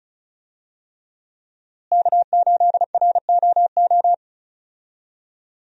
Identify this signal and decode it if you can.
Morse code, heard as a keyed tone, about 34 wpm, 700 Hz. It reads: K8ROO
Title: USA Call Signs - 5 Characters - Repeated 3x - 35wpm